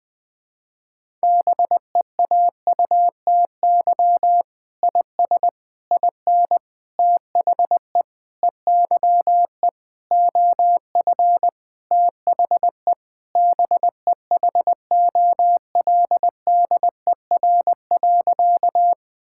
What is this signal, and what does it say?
Morse code, heard as a keyed tone, about 20 wpm, 700 Hz. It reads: BEAUTY IS IN THE EYE OF THE BEHOLDER.
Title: Sentences with 6-Word Phrases - 20wpm